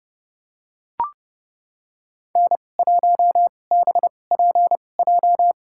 Morse code, heard as a keyed tone, about 32 wpm, spaced as for 22 wpm, 700 Hz. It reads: N1BPJ
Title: USA Call Signs - 5 Characters - 2x Character Spacing - 30wpm